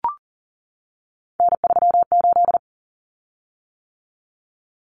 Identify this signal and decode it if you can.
Morse code, heard as a keyed tone, about 40 wpm, 700 Hz. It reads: N38
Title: Common Cut Numbers - 3-Digit 40wpm